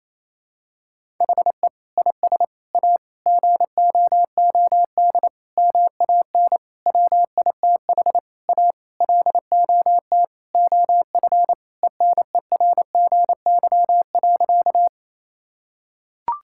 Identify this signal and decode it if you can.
Morse code, heard as a keyed tone, about 28 wpm, 700 Hz. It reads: HE IS A GOOD MAN WITH A LOT OF ENERGY.
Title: Sentences from Top 1000 Words - Review Part 6 - 28wpm